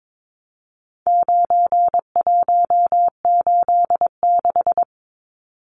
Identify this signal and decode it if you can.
Morse code, heard as a keyed tone, about 22 wpm, 700 Hz. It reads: 9186
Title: Numbers - 4-Digit 22wpm